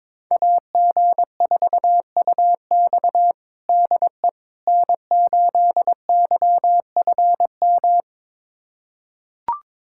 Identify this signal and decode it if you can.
Morse code, heard as a keyed tone, about 22 wpm, 700 Hz. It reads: AG4UX DE N8YFM
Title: USA Callsign DE Callsign - 5 Characters - 22wpm